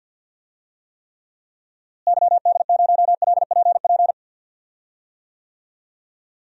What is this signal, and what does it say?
Morse code, heard as a keyed tone, about 51 wpm, 700 Hz. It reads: YD0LPP
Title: DXCC-100 Call Signs - 6 Characters - Repeated 2x - 50wpm